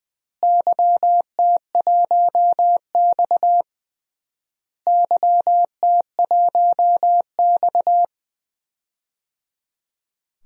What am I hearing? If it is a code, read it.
Morse code, heard as a keyed tone, about 21 wpm, 700 Hz. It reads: YT1X YT1X
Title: DXCC-100 Call Signs - 4 Characters - Repeated 3x - 20wpm